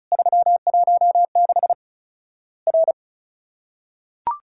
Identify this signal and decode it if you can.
Morse code, heard as a keyed tone, about 36 wpm, 700 Hz. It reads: 316 R
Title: Worked All Europe DX Contest - QTC Exchange - 35wpm